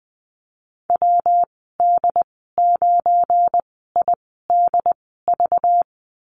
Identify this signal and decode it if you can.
Morse code, heard as a keyed tone, about 20 wpm, 700 Hz. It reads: W D 9 I D V
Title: 238 Lesson 38g - Letter q - Call Signs 20wpm